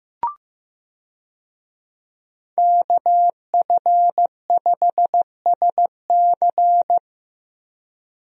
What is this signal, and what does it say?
Morse code, heard as a keyed tone, about 16 wpm, 700 Hz. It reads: KF5SC
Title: USA Call Signs - 5 Characters - Repeated 2x - 15wpm